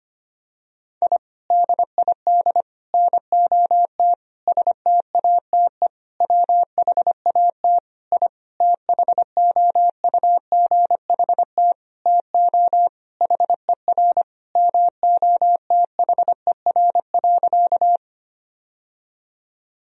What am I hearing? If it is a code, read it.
Morse code, heard as a keyed tone, about 25 wpm, 700 Hz. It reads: I DID NOT STATE WHAT I THOUGHT TO HER MOTHER.
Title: Sentences from Top 1000 Words - Review Part 2 - 25wpm